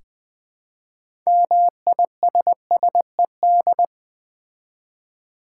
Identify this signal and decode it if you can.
Morse code, heard as a keyed tone, about 20 wpm, 700 Hz. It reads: MISSED
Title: Common QSO Phrases - 4x Word-Spacing - 20wpm